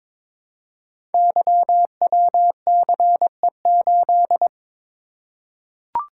Morse code, heard as a keyed tone, about 22 wpm, 700 Hz. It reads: YWCE8
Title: Code Groups - 5 Letter:Numbers - 22wpm